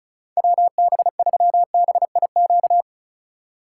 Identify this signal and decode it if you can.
Morse code, heard as a keyed tone, about 36 wpm, 700 Hz. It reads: WB3BIQ
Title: K1USN SST QSO Practice - 35wpm